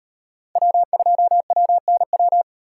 Morse code, heard as a keyed tone, about 36 wpm, 700 Hz. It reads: W2WNW